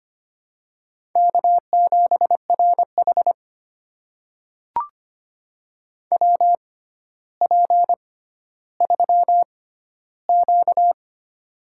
Morse code, heard as a keyed tone, about 25 wpm, spaced as for 13 wpm, 700 Hz. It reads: K7RH WP3Q
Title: USA Call Signs - 4 Characters - 6x Character Spacing - 25wpm